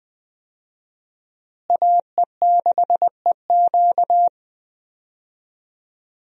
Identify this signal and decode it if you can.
Morse code, heard as a keyed tone, about 20 wpm, 700 Hz. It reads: AE6EQ
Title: USA Call Signs - 5 Characters - Repeated 3x - 20wpm